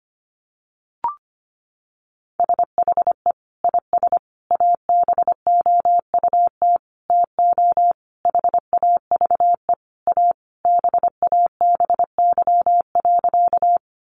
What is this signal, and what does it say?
Morse code, heard as a keyed tone, about 25 wpm, 700 Hz. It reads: SHE IS ABOUT TO HAVE A BABY.
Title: Sentences from Top 1300 Words 25wpm